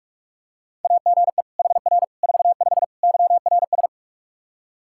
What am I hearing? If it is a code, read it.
Morse code, heard as a keyed tone, about 43 wpm, 700 Hz. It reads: AGE HR 45 YRS
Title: K1USN SST QSO Practice 45wpm